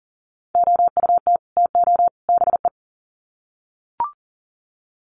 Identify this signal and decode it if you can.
Morse code, heard as a keyed tone, about 40 wpm, 700 Hz. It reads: OUT TO BE
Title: Sets of 3 Words 40wpm